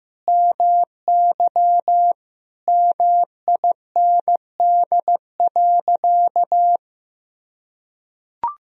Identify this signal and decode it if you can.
Morse code, heard as a keyed tone, about 15 wpm, 700 Hz. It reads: MY MIND.
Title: Sentences from Top 1000 Words - Review Part 1 - 15wpm